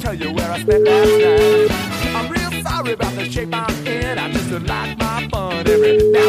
Ringback tone (Europe, 425 Hz), with noise.